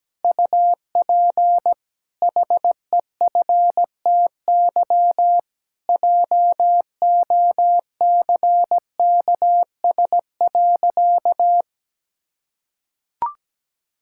Morse code, heard as a keyed tone, about 17 wpm, 700 Hz. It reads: UP HEFTY JOCKS.
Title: Pangram Challenge v2 - 17wpm